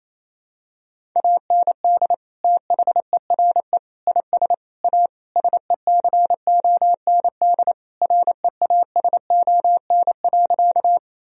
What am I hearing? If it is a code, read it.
Morse code, heard as a keyed tone, about 28 wpm, 700 Hz. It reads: AND THERE IS A SECOND REASON.
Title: Sentences from Top 1000 Words - Review Part 1 - 28wpm